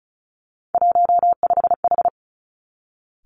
Morse code, heard as a keyed tone, about 36 wpm, 700 Hz. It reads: 15H
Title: Three Challenging Letter-Number Combinations 35wpm